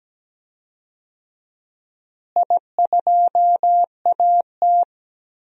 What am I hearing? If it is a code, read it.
Morse code, heard as a keyed tone, about 17 wpm, 700 Hz. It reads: I2AT
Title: DXCC-100 Call Signs - 4 Characters - Repeated 2x - 17wpm